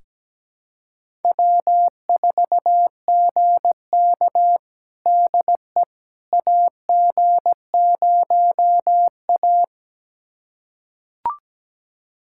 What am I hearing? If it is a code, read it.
Morse code, heard as a keyed tone, about 17 wpm, 700 Hz. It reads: W4GK DE AG0A
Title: USA Callsign DE Callsign - 4 Characters - 17wpm